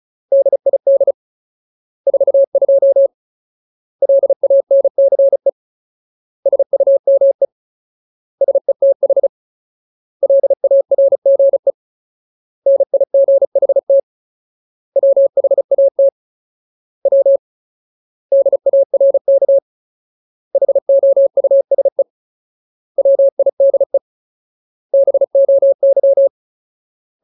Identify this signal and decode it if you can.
Morse code, heard as a keyed tone, about 34 wpm, 550 Hz. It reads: DID 42 LANCE SUME SETH LARGE NIGHT WHAT W DARK HOUSE WIDE BOY
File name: Ultimate Word List / 35WPM Ultimate